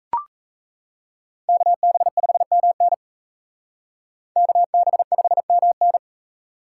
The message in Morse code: KB5MN KB5MN

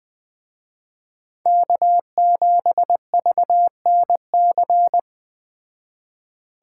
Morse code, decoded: K7VNC

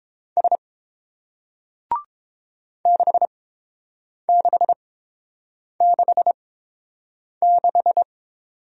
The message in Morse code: S 6666

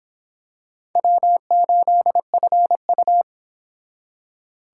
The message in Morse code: W8FU

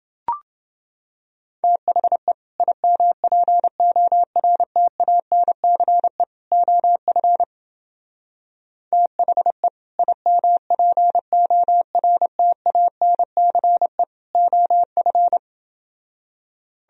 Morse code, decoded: THE IMPORTANCE OF THE IMPORTANCE OF

